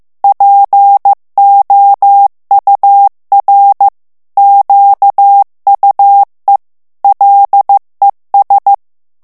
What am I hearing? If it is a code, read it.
Morse code, heard as a keyed tone, about 15 wpm, 800 Hz. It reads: POUR QUE LES
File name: morse mp3 / FRANCAIS TEXTE CLAIR 1 VITESSE 14 MOTS MINUTES